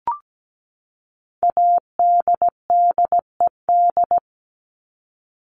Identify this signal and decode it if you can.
Morse code, heard as a keyed tone, about 17 wpm, 700 Hz. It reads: ADDED
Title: ED Suffix - Top 100 Words with Base Words 17wpm